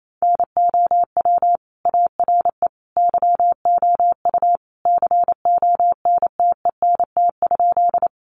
Morse code, heard as a keyed tone, about 28 wpm, 690 Hz. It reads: NOW ARE YOU CONTENT?
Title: Sentences from Top 900 Words 28wpm